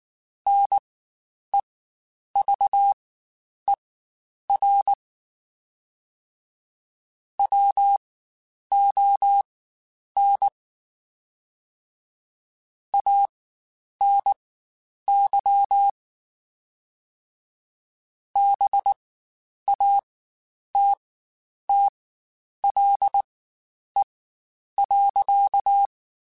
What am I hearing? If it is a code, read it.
Morse code, heard as a keyed tone, about 19 wpm, spaced as for 9 wpm, 800 Hz. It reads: NEVER WON ANY BATTLE.
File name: Quotes Spaced / Quotes Spaced-mp3s / Pessimism sp @18